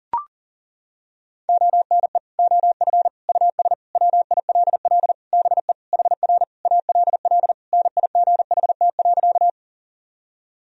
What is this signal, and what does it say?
Morse code, heard as a keyed tone, about 37 wpm, 700 Hz. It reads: ONE OF US WILL BE HR ALL NIGHT.
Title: Sentences from Top 300 Words and Abbreviations 40wpm